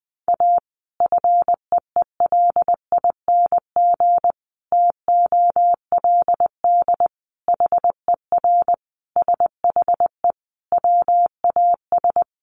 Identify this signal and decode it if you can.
Morse code, heard as a keyed tone, about 20 wpm, 700 Hz. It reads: A FEELING TOLD HER SHE WAS
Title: Sentences from Top 1200 Words 20wpm